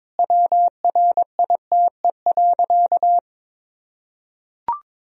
Morse code, heard as a keyed tone, about 22 wpm, 700 Hz. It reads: WRITE.
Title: Sentences from Top 1000 Words - Review Part 1 - 22wpm